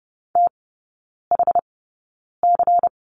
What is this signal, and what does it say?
Morse code, heard as a keyed tone, about 30 wpm, spaced as for 9 wpm, 690 Hz. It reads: THC
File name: Lesson 16e - Letter c - Three Characters 30wpm